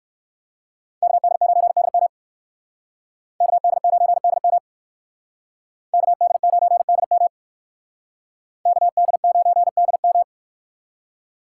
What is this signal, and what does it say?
Morse code, heard as a keyed tone, about 58 wpm, 700 Hz. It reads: KD9DM KD9DM KD9DM KD9DM